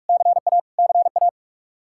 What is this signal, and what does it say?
Morse code, heard as a keyed tone, about 46 wpm, 700 Hz. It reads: CA CA
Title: Practice QSO - Exchange 1 - Beginning - 45wpm